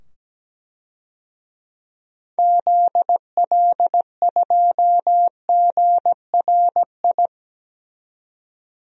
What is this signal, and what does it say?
Morse code, heard as a keyed tone, about 17 wpm, 700 Hz. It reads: ZL2GRI
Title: DXCC-100 Call Signs - 6 Characters - 17wpm